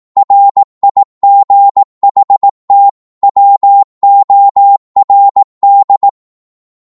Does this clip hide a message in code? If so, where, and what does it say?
Yes: Morse code from 0.2 to 6.1 s, heard as a keyed tone, about 18 wpm, 800 Hz. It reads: RIGHT WORD